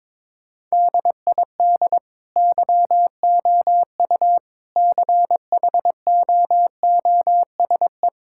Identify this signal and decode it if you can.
Morse code, heard as a keyed tone, about 22 wpm, 700 Hz. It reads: DID YOU CHOOSE
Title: Sentences from Top 1000 Words - Review Part 6 - 22wpm